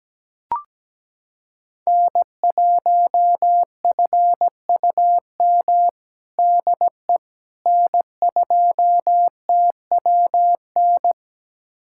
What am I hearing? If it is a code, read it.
Morse code, heard as a keyed tone, about 17 wpm, 700 Hz. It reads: N1FUM DE N2TWN
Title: USA Callsign DE Callsign - 5 Characters - 17wpm